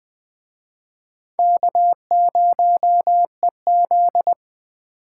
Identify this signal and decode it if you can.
Morse code, heard as a keyed tone, about 20 wpm, 700 Hz. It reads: K0EZ
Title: Speed Racing - USA 1x2 Call Signs 17wpm